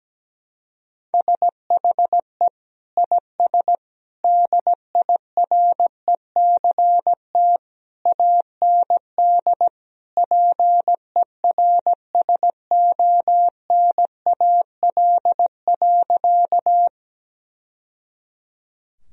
Morse code, heard as a keyed tone, about 17 wpm, 700 Hz. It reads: SHE IS DIRECT AND PERSONAL.
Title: Sentences from Top 1000 Words - Review Part 4 - 17wpm